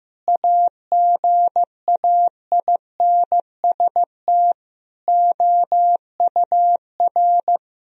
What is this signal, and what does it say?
Morse code, heard as a keyed tone, about 15 wpm, 700 Hz. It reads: AGAINST OUR
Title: Sentences from Top 1000 Words - Review Part 4 - 15wpm